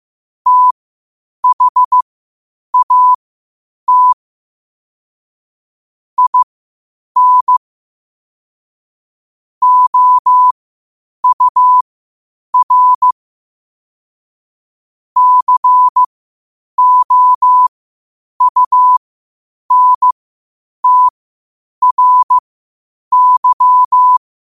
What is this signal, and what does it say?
Morse code, heard as a keyed tone, about 15 wpm, spaced as for 8 wpm, 1000 Hz. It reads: THAT IN OUR COUNTRY
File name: qotdfiles / 8wpm